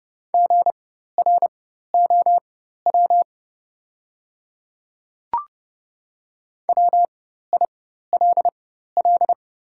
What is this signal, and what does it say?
Morse code, heard as a keyed tone, about 30 wpm, spaced as for 14 wpm, 700 Hz. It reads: GROW WILL
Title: 4x Character Spacing - Top 300 Words - 30wpm